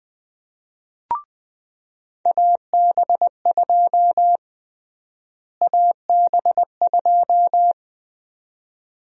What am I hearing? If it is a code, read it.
Morse code, heard as a keyed tone, about 20 wpm, 700 Hz. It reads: AB2 AB2
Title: USA Call Sign Prefixes - Repeated 2x - 20wpm